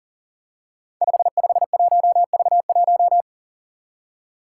Morse code, heard as a keyed tone, about 37 wpm, 700 Hz. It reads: 551V1
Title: Five Challenging Letter-Number Combinations 40wpm